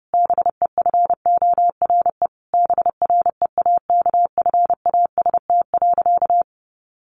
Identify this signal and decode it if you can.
Morse code, heard as a keyed tone, about 30 wpm, 700 Hz. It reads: BEFORE BREAKFAST.
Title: Sentences from Top 1500 Words 30wpm